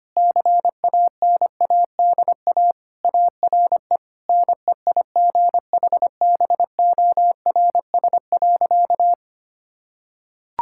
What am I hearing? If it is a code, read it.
Morse code, heard as a keyed tone, about 25 wpm, 700 Hz. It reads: CANADA ARE NEIGHBORS.
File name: practice / 25wpm / Sentences with 4-Word Phrases - 25wpm